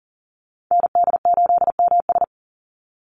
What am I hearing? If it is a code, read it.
Morse code, heard as a keyed tone, about 40 wpm, 700 Hz. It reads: ND8MS
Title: Call Signs Repeated 2x - 40wpm